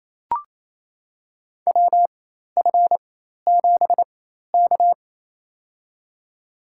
Morse code, heard as a keyed tone, about 27 wpm, spaced as for 13 wpm, 700 Hz. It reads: WF7K
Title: USA Call Signs - 4 Characters - 4x Character Spacing - 28wpm